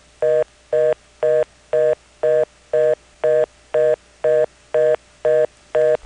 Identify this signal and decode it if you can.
Reorder tone (North America), 480 + 620 Hz.
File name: phonetrips / PrinceWmCoVA5HQ